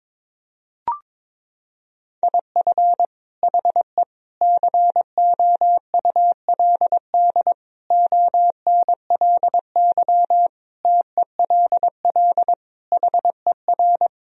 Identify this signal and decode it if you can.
Morse code, heard as a keyed tone, about 21 wpm, 700 Hz. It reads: IF HE COULD ONLY TELL HER
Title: Sentences from Top 1000 Words - Review Part 1 - 22wpm